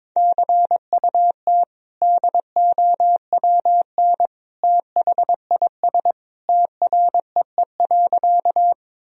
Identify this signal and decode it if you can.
Morse code, heard as a keyed tone, about 22 wpm, 700 Hz. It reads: CUT DOWN THIS TREE.
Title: Sentences from Top 1000 Words - Review Part 1 - 22wpm